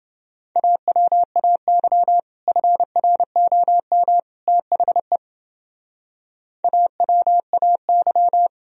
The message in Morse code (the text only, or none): AWAY FROM THE AWAY